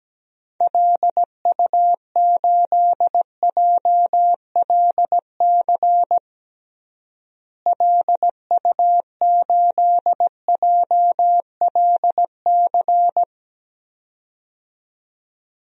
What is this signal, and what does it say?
Morse code, heard as a keyed tone, about 17 wpm, 700 Hz. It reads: LU8JLC LU8JLC
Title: DXCC-100 Call Signs - 6 Characters - Repeated 2x - 17wpm